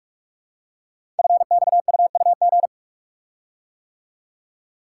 Morse code, heard as a keyed tone, about 44 wpm, 700 Hz. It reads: FXUUG
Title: Code Groups - 5 Letters - 45wpm